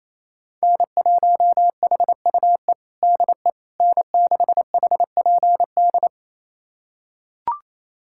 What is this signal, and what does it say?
Morse code, heard as a keyed tone, about 28 wpm, 700 Hz. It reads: N1HUE DE N6HPD
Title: USA Callsign DE Callsign 28wpm